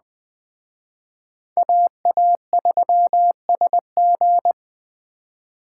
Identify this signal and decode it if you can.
Morse code, heard as a keyed tone, about 20 wpm, 700 Hz. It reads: AA3SG
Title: Speed Racing - USA 2x2 Call Signs 17wpm